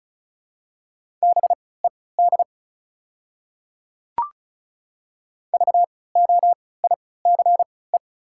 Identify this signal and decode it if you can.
Morse code, heard as a keyed tone, about 36 wpm, spaced as for 20 wpm, 700 Hz. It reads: BED VOICE
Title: 3x Character Spacing - Top 500 Words - 35wpm